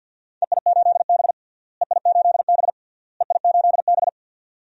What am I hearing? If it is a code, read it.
Morse code, heard as a keyed tone, about 50 wpm, 700 Hz. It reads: EI8B EI8B EI8B
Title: DXCC-100 Call Signs - 4 Characters - Repeated 3x - 50wpm